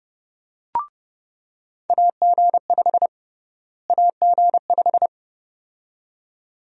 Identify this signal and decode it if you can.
Morse code, heard as a keyed tone, about 31 wpm, 700 Hz. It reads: AG5 AG5
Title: USA Call Sign Prefixes - Repeated 2x - 30wpm